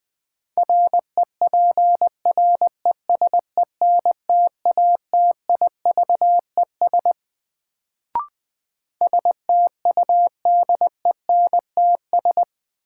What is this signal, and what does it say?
Morse code, heard as a keyed tone, about 20 wpm, 700 Hz. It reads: REPRESENTATIVES STUDENTS
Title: Sets of 5 Words Encore - New Word Drill - 20wpm